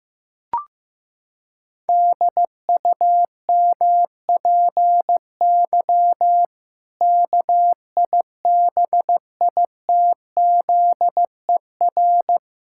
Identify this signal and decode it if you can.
Morse code, heard as a keyed tone, about 15 wpm, 700 Hz. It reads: DUMPY KIBITZER
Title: Pangram Challenge v2 - 15wpm